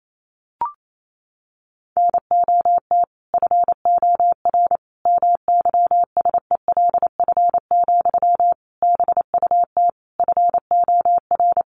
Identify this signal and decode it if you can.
Morse code, heard as a keyed tone, about 28 wpm, 700 Hz. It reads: NOT FOR MYSELF, BUT FOR
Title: Sentences from Top 1300 Words 28wpm